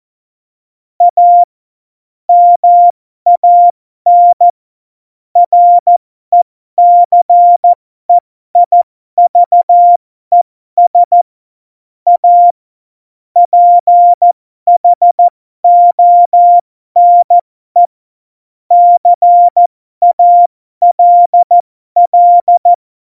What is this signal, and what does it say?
Morse code, heard as a keyed tone, about 14 wpm, spaced as for 12 wpm, 700 Hz. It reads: A MAN RECEIVES A PHONE CALL
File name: Morsekurs / Witze